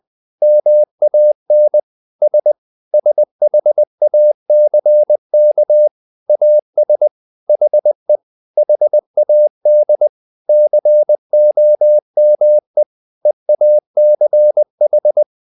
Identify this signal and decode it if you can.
Morse code, heard as a keyed tone, about 20 wpm, 600 Hz. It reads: MAN S SHACK AS HE HAD COME EACH
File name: the old man and the sea part 2 20wpm